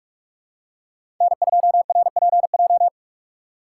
Morse code, heard as a keyed tone, about 43 wpm, 700 Hz. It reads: N2RPJ
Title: USA Call Signs - 2x Character Spacing - 45wpm